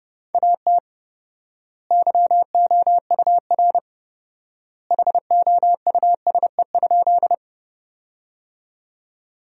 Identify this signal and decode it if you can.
Morse code, heard as a keyed tone, about 30 wpm, 700 Hz. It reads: AT YOUR HOUSE?
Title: 4x Word-Spacing - Sentences from Top 200 Words - 30wpm